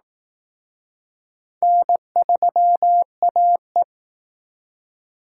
Morse code, heard as a keyed tone, about 19 wpm, 700 Hz. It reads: N3AE